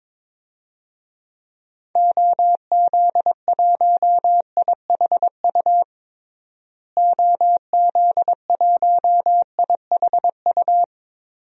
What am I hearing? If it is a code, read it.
Morse code, heard as a keyed tone, about 22 wpm, 700 Hz. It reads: OZ1IHU OZ1IHU
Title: DXCC-100 Call Signs - 6 Characters - Repeated 2x - 22wpm